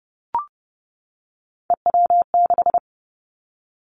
Morse code, heard as a keyed tone, about 30 wpm, 700 Hz. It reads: EW6
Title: Callsign Prefixes for Top 100 DXCC - 30wpm